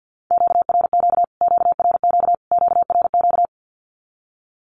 Morse code, heard as a keyed tone, about 50 wpm, 700 Hz. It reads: QRQ QRQ QRQ
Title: Q-Codes Repeated 3x 50wpm